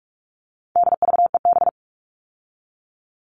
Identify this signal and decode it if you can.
Morse code, heard as a keyed tone, about 44 wpm, 700 Hz. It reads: DVEB
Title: All Cut Numbers - 4-Digit 45wpm